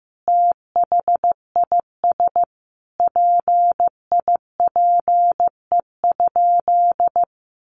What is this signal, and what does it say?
Morse code, heard as a keyed tone, about 14 wpm, 700 Hz. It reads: THIS PIPE?
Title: Sentences from Top 1500 Words 15wpm